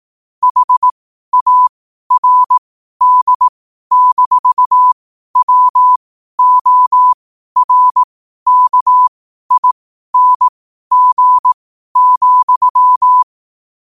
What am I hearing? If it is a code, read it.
Morse code, heard as a keyed tone, about 18 wpm, spaced as for 12 wpm, 1000 Hz. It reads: HARD-WORKING,
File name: qotdfiles / 12wpm